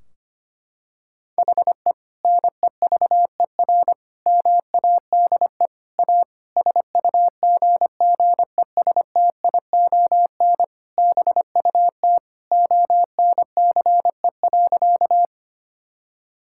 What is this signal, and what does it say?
Morse code, heard as a keyed tone, about 25 wpm, 700 Hz. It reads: HE NEVER MADE A SUGGESTION BUT ONCE.
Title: Sentences from Top 1600 Words 25wpm